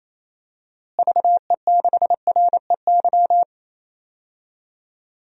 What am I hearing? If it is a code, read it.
Morse code, heard as a keyed tone, about 28 wpm, 700 Hz. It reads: VE6REY
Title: DXCC-100 Call Signs - 6 Characters - Repeated 3x - 28wpm